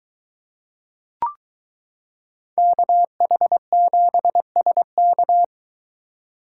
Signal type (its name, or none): Morse code